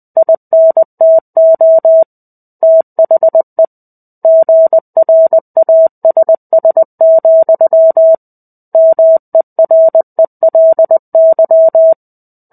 Morse code, heard as a keyed tone, about 20 wpm, 650 Hz. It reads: INTO THE GRASS, MERELY
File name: 05-Chapter-05 20 WPM